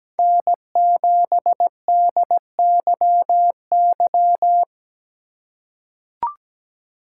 Morse code, heard as a keyed tone, about 17 wpm, 700 Hz. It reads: N7DYY
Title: USA Call Signs - 5 Characters - 2x Character Spacing - 17wpm